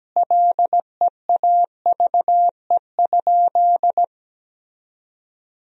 Morse code, heard as a keyed tone, about 17 wpm, 700 Hz. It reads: LEAVE?
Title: 4x Word-Spacing - Sentences from Top 400 Words - 17wpm